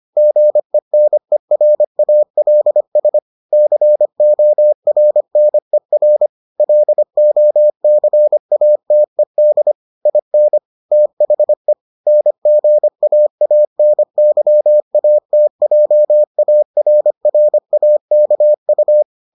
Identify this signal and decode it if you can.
Morse code, heard as a keyed tone, about 25 wpm, 600 Hz. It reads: GENERALS CORNER LOCATED IN THE NGAANYATJARRAKU